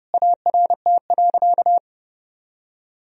Morse code, heard as a keyed tone, about 30 wpm, 700 Hz. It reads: ART.
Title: Sentences from Top 1000 Words - Review Part 3 - 30wpm